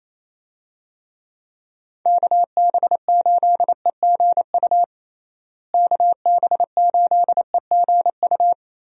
Morse code, heard as a keyed tone, about 28 wpm, 700 Hz. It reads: KB8EGU KB8EGU
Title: USA Call Signs - 6 Characters - Repeated 3x - 28wpm